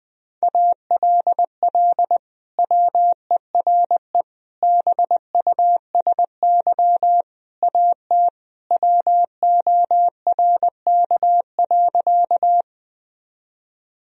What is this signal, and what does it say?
Morse code, heard as a keyed tone, about 20 wpm, 700 Hz. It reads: ALL WERE BUSY AT WORK.
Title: Sentences from Top 1000 Words - Review Part 3 - 20wpm